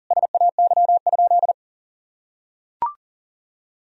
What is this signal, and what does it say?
Morse code, heard as a keyed tone, about 40 wpm, 700 Hz. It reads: SAY?
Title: Sentences from Top 1000 Words - Review Part 1 - 40wpm